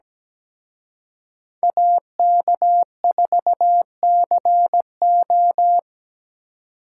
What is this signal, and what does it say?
Morse code, heard as a keyed tone, about 20 wpm, 700 Hz. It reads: AK4CO
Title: Speed Racing - USA 2x2 Call Signs 17wpm